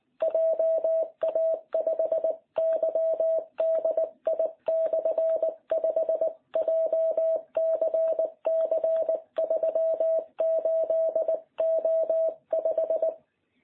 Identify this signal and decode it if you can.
Morse code, heard as a keyed tone, about 19 wpm, 660 Hz. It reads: JA5YDI/5 JCC 38O5